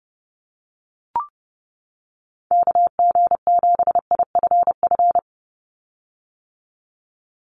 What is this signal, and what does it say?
Morse code, heard as a keyed tone, about 28 wpm, 700 Hz. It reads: KG7IFF
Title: USA Call Signs 30wpm